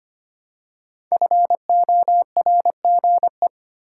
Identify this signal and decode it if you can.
Morse code, heard as a keyed tone, about 25 wpm, 700 Hz. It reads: FORGE
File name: 5 Letter Words - First 3 Letters Given - 25wpm